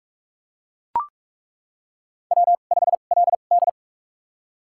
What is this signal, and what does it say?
Morse code, heard as a keyed tone, about 43 wpm, spaced as for 32 wpm, 700 Hz. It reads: W5LD
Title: USA Call Signs - 4 Characters - 2x Character Spacing - 45wpm